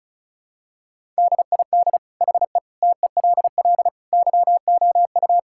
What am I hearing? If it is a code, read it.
Morse code, heard as a keyed tone, about 34 wpm, 700 Hz. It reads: DID HE TELL YOU
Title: Sentences from Top 1600 Words 35wpm